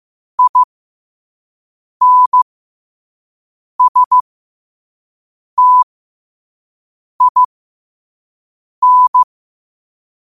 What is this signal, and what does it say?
Morse code, heard as a keyed tone, about 15 wpm, spaced as for 5 wpm, 1000 Hz. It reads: INSTIN
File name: qotdfiles / 5wpm